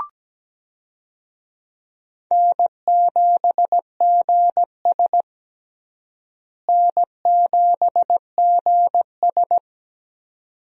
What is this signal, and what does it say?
Morse code, heard as a keyed tone, about 17 wpm, 700 Hz. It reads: N7GS N7GS
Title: USA Call Signs - 4 Characters - Repeated 2x - 17wpm